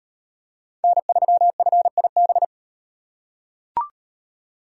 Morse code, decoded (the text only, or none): N3FIB